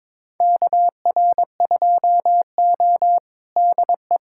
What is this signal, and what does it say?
Morse code, heard as a keyed tone, about 22 wpm, 700 Hz. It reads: KR2O DE
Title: USA Callsign DE Callsign - 4 Characters - 22wpm